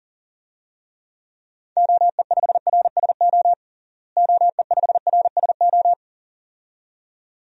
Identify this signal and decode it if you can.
Morse code, heard as a keyed tone, about 40 wpm, 700 Hz. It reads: OE5RSO OE5RSO
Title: DXCC-100 Call Signs - 6 Characters - Repeated 2x - 40wpm